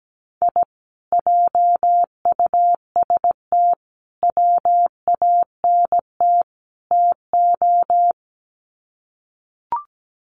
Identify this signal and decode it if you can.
Morse code, heard as a keyed tone, about 17 wpm, 700 Hz. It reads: I JUST WANT TO
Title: Sets of 4 Words 17wpm